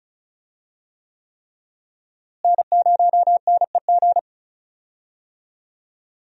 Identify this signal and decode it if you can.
Morse code, heard as a keyed tone, about 36 wpm, 700 Hz. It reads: N0NEG
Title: USA Call Signs - 5 Characters - Repeated 3x - 35wpm